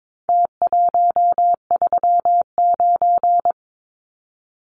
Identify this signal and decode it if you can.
Morse code, heard as a keyed tone, about 22 wpm, 700 Hz. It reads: T139
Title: Numbers - Contest Style - 4-Digit 22wpm